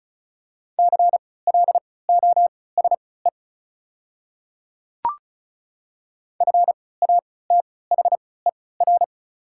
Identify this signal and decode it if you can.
Morse code, heard as a keyed tone, about 36 wpm, spaced as for 20 wpm, 690 Hz. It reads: CLOSE FATHER
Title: 3x Character Spacing - Top 500 Words - 35wpm